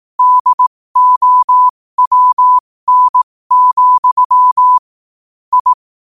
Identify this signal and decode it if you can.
Morse code, heard as a keyed tone, about 18 wpm, spaced as for 15 wpm, 1000 Hz. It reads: DOWN, I
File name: qotdfiles / 15wpm